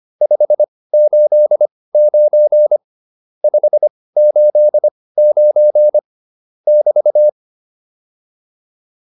Morse code, heard as a keyed tone, about 25 wpm, spaced as for 18 wpm, 600 Hz. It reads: 589 589 <BT>